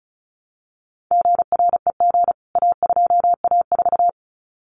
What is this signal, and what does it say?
Morse code, heard as a keyed tone, about 34 wpm, 700 Hz. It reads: GREG A2A4
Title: CWT Exchange 35wpm